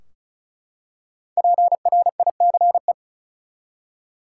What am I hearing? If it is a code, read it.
Morse code, heard as a keyed tone, about 36 wpm, 700 Hz. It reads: PRICE
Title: Five Letter Words - 35wpm